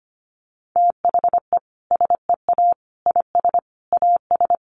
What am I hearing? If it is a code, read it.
Morse code, heard as a keyed tone, about 25 wpm, 700 Hz. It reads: THE SEA IS AS
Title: Sentences from Top 1400 Words 25wpm